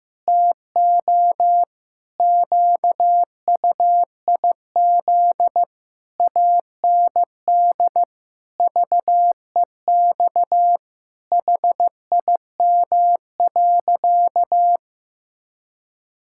Morse code, heard as a keyed tone, about 15 wpm, 700 Hz. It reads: TO QUIZ AND VEX HIM.
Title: Pangram Challenge v2 - 15wpm